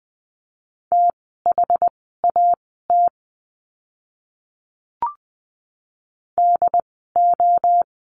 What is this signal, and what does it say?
Morse code, heard as a keyed tone, about 20 wpm, spaced as for 14 wpm, 700 Hz. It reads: THAT DO
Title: 139 Lesson 23f - Letter f - Words 20wpm